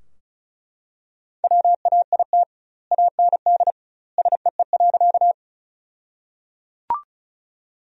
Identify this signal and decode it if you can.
Morse code, heard as a keyed tone, about 34 wpm, 700 Hz. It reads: WAIT AND SEE.
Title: 2x Word-Spacing - Sentences from Top 500 Words - 35wpm